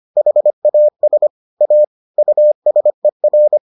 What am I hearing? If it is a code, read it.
Morse code, heard as a keyed tone, about 25 wpm, 600 Hz. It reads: HAS A USER